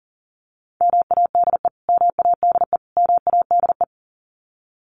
Morse code, heard as a keyed tone, about 40 wpm, 700 Hz. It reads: MADE MADE MADE
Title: Top 200 Words Repeated 40wpm